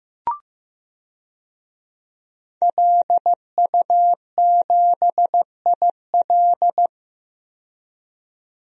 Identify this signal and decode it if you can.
Morse code, heard as a keyed tone, about 15 wpm, 700 Hz. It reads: LU7IL
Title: DXCC-100 Call Signs - 5 Characters - 15wpm